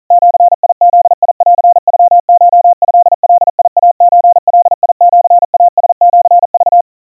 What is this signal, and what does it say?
Morse code (Japanese Wabun code), heard as a keyed tone, about 40 wpm, 700 Hz. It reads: ジブンノコトガイソガシイラシク
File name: kawa / Morse sample / kamyu 150 cpm 700 Hz Crispy